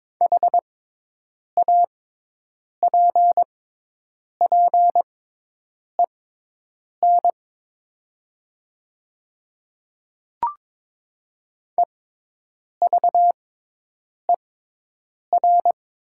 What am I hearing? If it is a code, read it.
Morse code, heard as a keyed tone, about 22 wpm, spaced as for 8 wpm, 700 Hz. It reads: HAPPEN EVER
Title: 6x Character Spacing - Top 500 Words - 22wpm